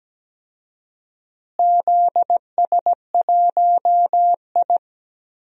Morse code, heard as a keyed tone, about 17 wpm, 700 Hz. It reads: ZS1I
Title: DXCC-100 Call Signs - 4 Characters - Repeated 2x - 17wpm